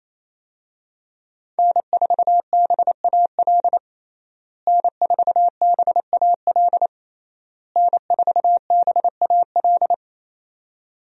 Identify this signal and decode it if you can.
Morse code, heard as a keyed tone, about 28 wpm, 700 Hz. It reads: N4BAL N4BAL N4BAL
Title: USA Call Signs - 5 Characters - Repeated 3x - 28wpm